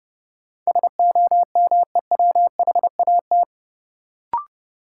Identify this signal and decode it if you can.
Morse code, heard as a keyed tone, about 29 wpm, 700 Hz. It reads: SOMEWHAT
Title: Sentences from Top 1300 Words - New Word Drill Alt - 30wpm